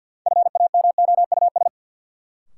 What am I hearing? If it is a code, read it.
Morse code, heard as a keyed tone, about 53 wpm, 700 Hz. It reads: FAMOUS
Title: Sentences from Top 1500 Words - New Word Drill Alt - 50wpm